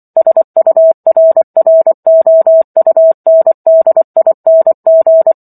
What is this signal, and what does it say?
Morse code, heard as a keyed tone, about 24 wpm, 650 Hz. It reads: SURROUNDING
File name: Chapter 15 - 24 WPM